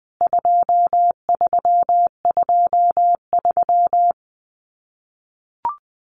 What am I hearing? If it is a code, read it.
Morse code, heard as a keyed tone, about 20 wpm, 700 Hz. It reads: 2323
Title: CWT Exchange 20wpm